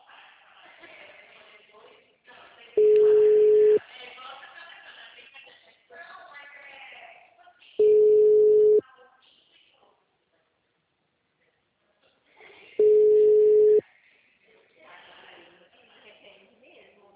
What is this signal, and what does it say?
Ringback tone (Europe, 425 Hz), with speech.